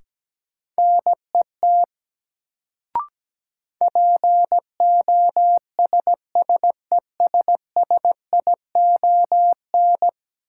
Morse code, heard as a keyed tone, about 17 wpm, 700 Hz. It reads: NET POSSESSION
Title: Sentences from Top 1400 Words - New Word Drill Alt - 17wpm